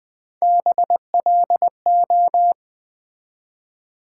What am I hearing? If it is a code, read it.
Morse code, heard as a keyed tone, about 20 wpm, 700 Hz. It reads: BLO
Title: Build A Word - 1 Letter At A Time - 4-Letter Words - 20wpm